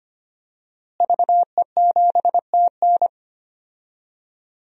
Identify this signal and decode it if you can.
Morse code, heard as a keyed tone, about 25 wpm, 700 Hz. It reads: VE7TN
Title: DXCC-100 Call Signs - 5 Characters - Repeated 3x - 25wpm